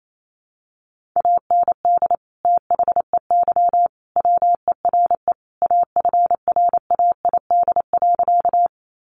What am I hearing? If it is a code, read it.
Morse code, heard as a keyed tone, about 28 wpm, 700 Hz. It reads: AND THEY WERE AFRAID.
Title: Sentences from Top 1500 Words 28wpm